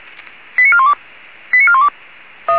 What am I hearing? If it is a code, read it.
Five-tone selective call (ZVEI), heard as quick runs of five tones, tones of about 70 ms. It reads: 89412 89412